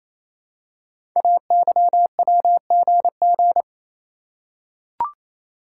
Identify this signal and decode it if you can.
Morse code, heard as a keyed tone, about 27 wpm, 700 Hz. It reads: AYWGG
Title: Code Groups - 5 Letters - 28wpm